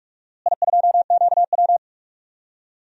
Morse code, heard as a keyed tone, about 41 wpm, 690 Hz. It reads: I2QW
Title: DXCC-100 Call Signs - 4 Characters - Repeated 3x - 45wpm